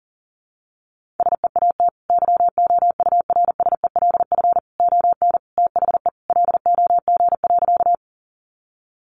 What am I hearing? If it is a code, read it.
Morse code, heard as a keyed tone, about 40 wpm, 700 Hz. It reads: SEAT YOURSELF ON THE LOG.
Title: Sentences from Top 1500 Words 40wpm